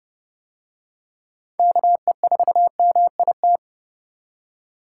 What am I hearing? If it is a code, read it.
Morse code, heard as a keyed tone, about 28 wpm, 700 Hz. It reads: KE4MIT